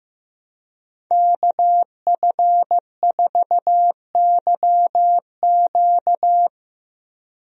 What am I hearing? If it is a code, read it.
Morse code, heard as a keyed tone, about 15 wpm, 700 Hz. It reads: KF4YQ
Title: USA Call Signs - 5 Characters - Repeated 2x - 15wpm